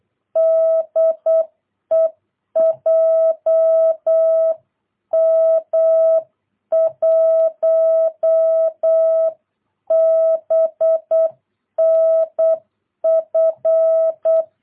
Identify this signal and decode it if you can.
Morse code, heard as a keyed tone, about 8 wpm, 660 Hz. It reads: DE JM1BNF
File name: jh8cbh / 2022cw